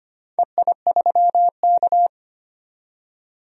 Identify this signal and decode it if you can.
Morse code, heard as a keyed tone, about 25 wpm, 700 Hz. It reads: EI3K
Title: DXCC-100 Call Signs - 4 Characters - 25wpm